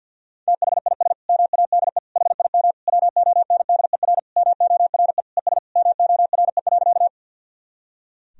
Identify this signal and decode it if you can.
Morse code, heard as a keyed tone, about 52 wpm, 700 Hz. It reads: THIS MADE HIM WONDER MORE ES MORE.
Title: Sentences from Top 500 Words and Abbreviations 50wpm